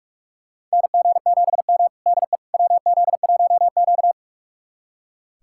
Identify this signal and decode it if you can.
Morse code, heard as a keyed tone, about 43 wpm, 700 Hz. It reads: NG7M DE WZ1Q
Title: USA Callsign DE Callsign - 4 Characters - 45wpm